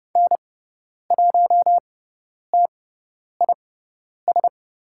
Morse code, heard as a keyed tone, about 30 wpm, spaced as for 10 wpm, 690 Hz. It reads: N1TIS
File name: Lesson 17g - Letter u - Call Signs 30wpm